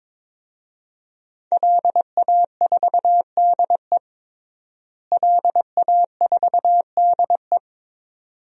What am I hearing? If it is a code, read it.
Morse code, heard as a keyed tone, about 21 wpm, 700 Hz. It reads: LA4DE LA4DE
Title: DXCC-100 Call Signs - 5 Characters - Repeated 2x - 22wpm